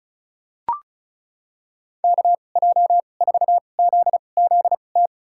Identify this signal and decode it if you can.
Morse code, heard as a keyed tone, about 34 wpm, spaced as for 25 wpm, 700 Hz. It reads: KJ4ZZT
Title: USA Call Signs - 2x Character Spacing - 35wpm